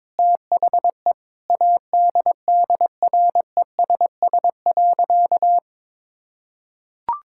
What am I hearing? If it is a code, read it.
Morse code, heard as a keyed tone, about 22 wpm, 700 Hz. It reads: THE ADDRESS.
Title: Sentences from Top 1000 Words - Review Part 6 - 22wpm